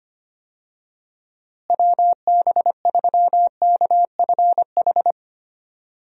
Morse code, heard as a keyed tone, about 25 wpm, 700 Hz. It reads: WB3KFH
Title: USA Call Signs - 6 Characters - Repeated 2x - 25wpm